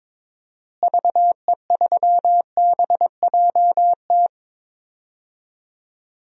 Morse code, heard as a keyed tone, about 22 wpm, 700 Hz. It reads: VE3BJT
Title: DXCC-100 Call Signs - 6 Characters - Repeated 3x - 22wpm